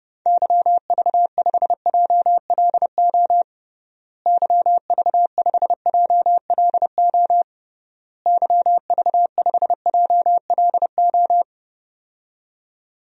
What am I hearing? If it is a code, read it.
Morse code, heard as a keyed tone, about 30 wpm, 700 Hz. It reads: YV5JLO YV5JLO YV5JLO
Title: DXCC-100 Call Signs - 6 Characters - Repeated 3x - 30wpm